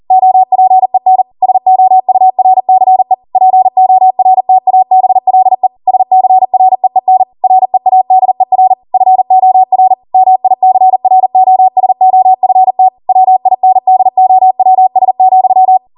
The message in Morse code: OPEN SOURCE PORTABLE SCREEN READER FOR MICROSOFT WINDOWS,